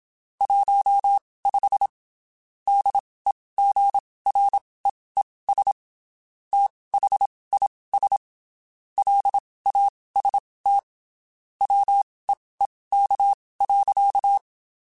Morse code, heard as a keyed tone, about 27 wpm, spaced as for 19 wpm, 800 Hz. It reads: 15 DEGREES THIS LAST WEEK.
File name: (spaced) / QSOs Spaced-mp3s / WD0FBT @25 sp